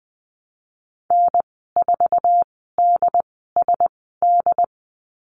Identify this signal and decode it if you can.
Morse code, heard as a keyed tone, about 20 wpm, spaced as for 15 wpm, 700 Hz. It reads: N4DSD
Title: 122 Lesson 20g - Number 3 - Call Signs 20wpm